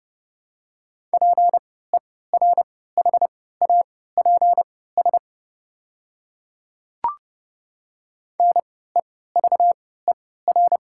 Morse code, heard as a keyed tone, about 32 wpm, spaced as for 17 wpm, 700 Hz. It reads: PERHAPS NEVER
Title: 3x Character Spacing - Top 500 Words - 30wpm